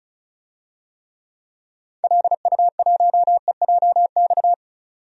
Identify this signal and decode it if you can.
Morse code, heard as a keyed tone, about 36 wpm, 700 Hz. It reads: LU1EJX
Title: DXCC-100 Call Signs - 6 Characters - 35wpm